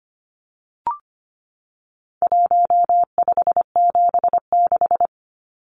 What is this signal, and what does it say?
Morse code, heard as a keyed tone, about 25 wpm, 700 Hz. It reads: 1576
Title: Numbers - 4-Digit 25wpm